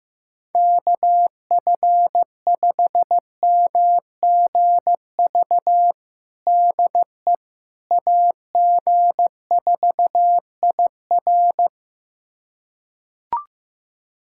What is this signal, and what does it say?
Morse code, heard as a keyed tone, about 16 wpm, 700 Hz. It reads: KF5MGV DE AG4IR
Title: USA Callsign DE Callsign 15wpm